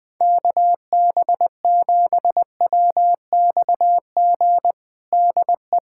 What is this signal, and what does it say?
Morse code, heard as a keyed tone, about 20 wpm, 700 Hz. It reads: KB7WXG DE
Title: USA Callsign DE Callsign 20wpm